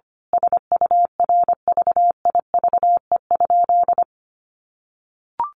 Morse code, heard as a keyed tone, about 25 wpm, 700 Hz. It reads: SURVIVE?
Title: A Sound Of Thunder 25wpm